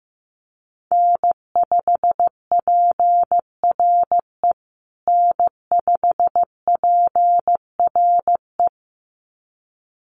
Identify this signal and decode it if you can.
Morse code, heard as a keyed tone, about 15 wpm, 700 Hz. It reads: N5PRE N5PRE
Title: USA Call Signs Repeated 2x 15wpm